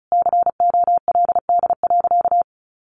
Morse code, heard as a keyed tone, about 34 wpm, 700 Hz. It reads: COLD.
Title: Sentences from Top 1100 Words 35wpm